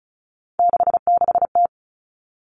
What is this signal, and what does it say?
Morse code, heard as a keyed tone, about 34 wpm, 700 Hz. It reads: 66T
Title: Common Cut Numbers - 3-Digit 35wpm